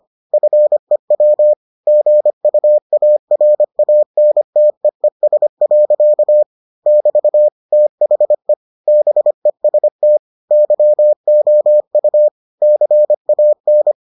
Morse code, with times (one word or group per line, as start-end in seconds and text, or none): 0.3-1.5 FEW
1.9-6.4 GUARANTEES.
6.9-7.5 <BT>
7.7-8.5 THE
8.9-10.2 BEST
10.5-12.3 YOU
12.6-13.9 CAN